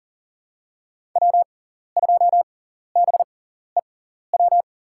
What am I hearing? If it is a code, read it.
Morse code, heard as a keyed tone, about 40 wpm, spaced as for 14 wpm, 700 Hz. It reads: W2BEW
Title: USA Call Signs - 5 Characters - 6x Character Spacing - 40wpm